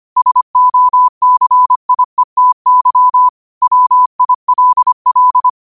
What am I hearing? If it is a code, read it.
Morse code, heard as a keyed tone, about 25 wpm, 1000 Hz. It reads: SOCIETY WILL
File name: qotdfiles / 25wpm